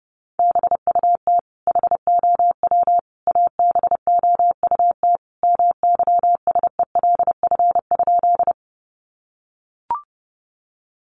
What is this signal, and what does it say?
Morse code, heard as a keyed tone, about 30 wpm, 700 Hz. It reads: BUT HOW ABOUT MYSELF?
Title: Sentences from Top 1300 Words 30wpm